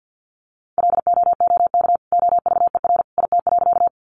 Morse code, heard as a keyed tone, about 50 wpm, 700 Hz. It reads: LOOK OVER IT.
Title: Sentences from Top 100 Words 50wpm